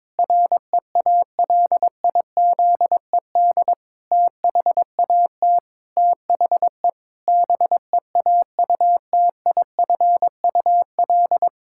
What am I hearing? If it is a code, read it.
Morse code, heard as a keyed tone, about 22 wpm, 700 Hz. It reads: REALIZED THAT THE BEAUTIFUL
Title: Pangram Challenge v2 - 22wpm